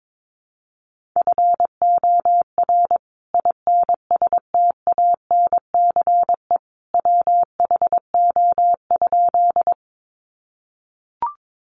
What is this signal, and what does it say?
Morse code, heard as a keyed tone, about 22 wpm, 700 Hz. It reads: FOR INSTANCE WHO?
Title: Sentences from Top 800 Words 22wpm